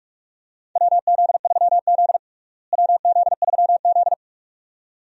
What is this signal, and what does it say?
Morse code, heard as a keyed tone, about 46 wpm, 700 Hz. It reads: WZ3Z WZ3Z
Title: USA Call Signs - 4 Characters - Repeated 2x - 45wpm